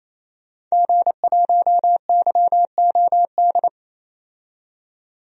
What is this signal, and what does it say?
Morse code, heard as a keyed tone, about 28 wpm, 700 Hz. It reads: G1YOD
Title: DXCC-100 Call Signs - 5 Characters - 28wpm